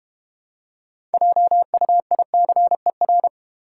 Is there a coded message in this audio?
Yes: Morse code from 1.1 to 3.3 s, heard as a keyed tone, about 32 wpm, 690 Hz. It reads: JUICER